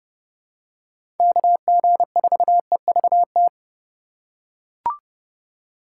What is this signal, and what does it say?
Morse code, heard as a keyed tone, about 30 wpm, 700 Hz. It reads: KG4EVT
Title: USA Call Signs - 6 Characters - 2x Character Spacing - 30wpm